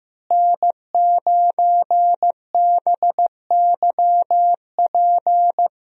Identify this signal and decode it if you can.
Morse code, heard as a keyed tone, about 15 wpm, 700 Hz. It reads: N9BYP
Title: USA Call Signs - 5 Characters - Repeated 2x - 15wpm